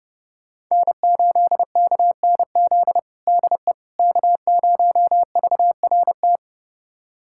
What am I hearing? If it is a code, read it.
Morse code, heard as a keyed tone, about 30 wpm, 700 Hz. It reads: N8KNZ DE K0VRT
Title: USA Callsign DE Callsign - 5 Characters - 30wpm